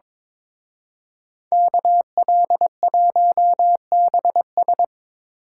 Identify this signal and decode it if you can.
Morse code, heard as a keyed tone, about 26 wpm, 700 Hz. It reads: KL1BS
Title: Speed Racing - USA 2x2 Call Signs 22wpm